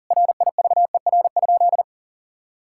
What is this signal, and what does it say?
Morse code, heard as a keyed tone, about 40 wpm, 690 Hz. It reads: RIVER?
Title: Sentences from Top 1000 Words - Review Part 6 - 40wpm